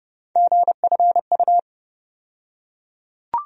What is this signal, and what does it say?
Morse code, heard as a keyed tone, about 30 wpm, 700 Hz. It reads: GFU
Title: USA Call Sign Postfixes - 3 Letters 30wpm